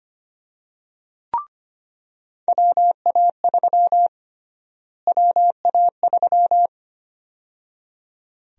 Morse code, heard as a keyed tone, about 25 wpm, 700 Hz. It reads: WA3 WA3
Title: USA Call Sign Prefixes - Repeated 2x - 25wpm